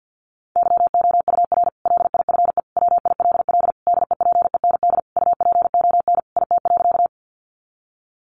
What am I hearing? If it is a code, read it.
Morse code, heard as a keyed tone, about 50 wpm, 700 Hz. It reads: YOUR LIFE WILL DEPEND UPON IT.
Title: Sentences from Top 1300 Words 50wpm